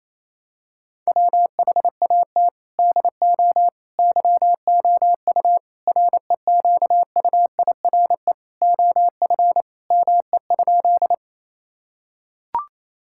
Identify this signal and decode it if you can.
Morse code, heard as a keyed tone, about 28 wpm, 700 Hz. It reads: WHAT DO YOU REQUIRE OF ME?
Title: Sentences from Top 1000 Words - Review Part 6 - 28wpm